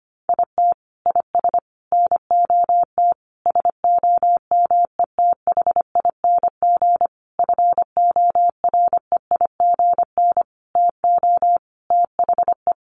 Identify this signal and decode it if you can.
Morse code, heard as a keyed tone, about 25 wpm, 700 Hz. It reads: IT IS NOT SOMETHING FOREIGN TO THE
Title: Sentences from Top 1400 Words 25wpm